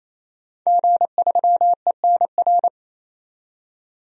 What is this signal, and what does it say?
Morse code, heard as a keyed tone, about 28 wpm, 700 Hz. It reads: G3ENR
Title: DXCC-100 Call Signs - 5 Characters - 28wpm